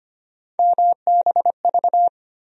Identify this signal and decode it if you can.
Morse code, heard as a keyed tone, about 25 wpm, 700 Hz. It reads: MBV
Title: USA Call Sign Postfixes - 3 Letters 25wpm